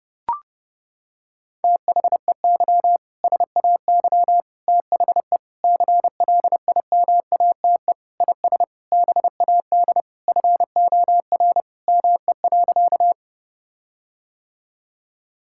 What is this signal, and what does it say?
Morse code, heard as a keyed tone, about 32 wpm, 700 Hz. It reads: THEY SAY THE CLIMATE IS BAD FOR ME.
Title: Sentences from Top 1600 Words 30wpm